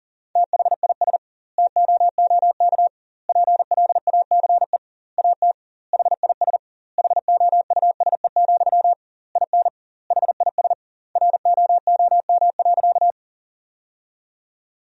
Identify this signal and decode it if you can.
Morse code, heard as a keyed tone, about 40 wpm, 700 Hz. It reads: THIS TOOK PLACE AT HIS HOUSE, IN HIS ROOM.
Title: 2x Word-Spacing - Sentences from Top 300 Words - 40wpm